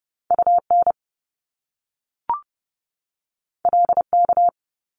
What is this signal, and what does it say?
Morse code, heard as a keyed tone, about 30 wpm, 700 Hz. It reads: UN LK
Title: Top Two Letter Combinations in Words 30wpm